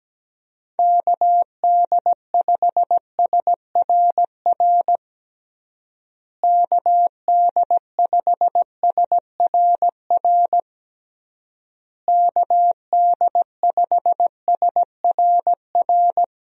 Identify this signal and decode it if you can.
Morse code, heard as a keyed tone, about 17 wpm, 700 Hz. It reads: KD5SRR KD5SRR KD5SRR
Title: USA Call Signs - 6 Characters - Repeated 3x - 17wpm